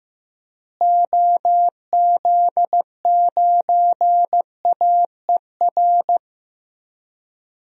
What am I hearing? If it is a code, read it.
Morse code, heard as a keyed tone, about 16 wpm, 700 Hz. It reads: OZ9AER
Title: DXCC-100 Call Signs - 6 Characters - Repeated 2x - 15wpm